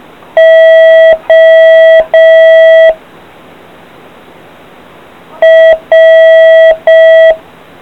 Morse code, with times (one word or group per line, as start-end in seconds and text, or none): 0.4-2.9 O
5.4-7.3 R